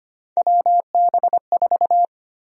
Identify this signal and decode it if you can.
Morse code, heard as a keyed tone, about 25 wpm, 700 Hz. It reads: WB4
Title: USA Call Sign Prefixes - 25wpm